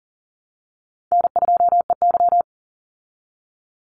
Morse code, heard as a keyed tone, about 40 wpm, 700 Hz. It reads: N2EY
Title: Easy Call Signs Repeated 3x - 40wpm